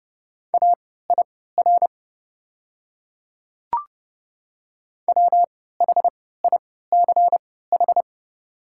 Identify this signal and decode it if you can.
Morse code, heard as a keyed tone, about 30 wpm, spaced as for 17 wpm, 700 Hz. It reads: AIR WHICH
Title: 3x Character Spacing - Top 200 Words - 30wpm